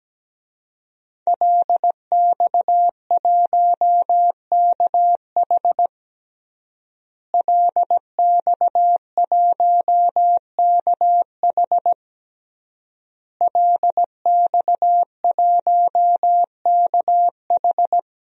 Morse code, heard as a keyed tone, about 17 wpm, 700 Hz. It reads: LX1KH LX1KH LX1KH
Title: DXCC-100 Call Signs - 5 Characters - Repeated 3x - 17wpm